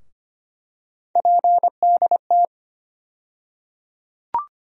Morse code, heard as a keyed tone, about 25 wpm, 700 Hz. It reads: PDT
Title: USA Call Sign Postfixes - 3 Letters 25wpm